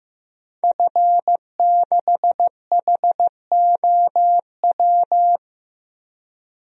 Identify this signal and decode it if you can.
Morse code, heard as a keyed tone, about 15 wpm, 700 Hz. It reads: F6HOW
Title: DXCC-100 Call Signs - 5 Characters - Repeated 2x - 15wpm